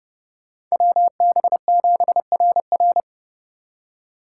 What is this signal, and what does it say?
Morse code, heard as a keyed tone, about 35 wpm, 700 Hz. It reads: WB7RR
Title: Speed Racing - USA 2x2 Call Signs 30wpm